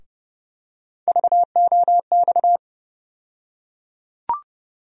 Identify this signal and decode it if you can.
Morse code, heard as a keyed tone, about 30 wpm, 700 Hz. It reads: VOX
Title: USA Call Sign Postfixes - 3 Letters 30wpm